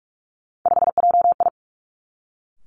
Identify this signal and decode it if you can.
Morse code, heard as a keyed tone, about 46 wpm, 700 Hz. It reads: 5JI
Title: Three Challenging Letter-Number Combinations 45wpm